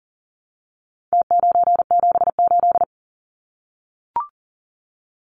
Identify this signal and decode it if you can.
Morse code, heard as a keyed tone, about 40 wpm, 700 Hz. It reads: T978
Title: Numbers - Contest Style - 4-Digit 40wpm